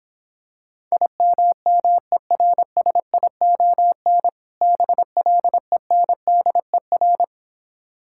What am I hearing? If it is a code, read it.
Morse code, heard as a keyed tone, about 26 wpm, 700 Hz. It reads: IMMERSION BLENDER